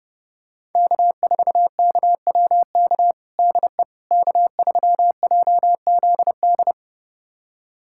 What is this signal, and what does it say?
Morse code, heard as a keyed tone, about 30 wpm, 700 Hz. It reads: K4KWK DE K3JZD
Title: USA Callsign DE Callsign - 5 Characters - 30wpm